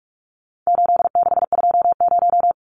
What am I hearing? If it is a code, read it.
Morse code, heard as a keyed tone, about 44 wpm, 700 Hz. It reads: 8620
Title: Numbers - 4-Digit 45wpm